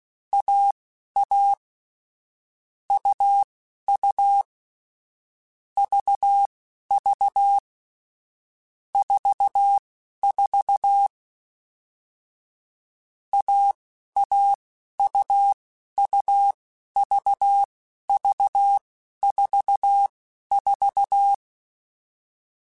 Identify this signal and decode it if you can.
Morse code, heard as a keyed tone, about 16 wpm, spaced as for 11 wpm, 800 Hz. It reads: AA UU VV 44 AAUUVV44